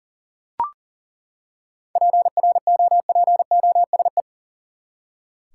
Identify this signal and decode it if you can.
Morse code, heard as a keyed tone, about 40 wpm, 700 Hz. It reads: PROPOSE
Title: Seven Letter Words - 40wpm